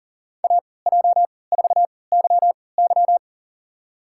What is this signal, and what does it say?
Morse code, heard as a keyed tone, about 40 wpm, spaced as for 23 wpm, 700 Hz. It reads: AJ4YY
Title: USA Call Signs - 5 Characters - 3x Character Spacing - 40wpm